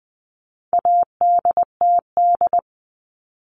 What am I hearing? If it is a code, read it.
Morse code, heard as a keyed tone, about 20 wpm, 700 Hz. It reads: ADTD
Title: All Cut Numbers - 4-Digit 20wpm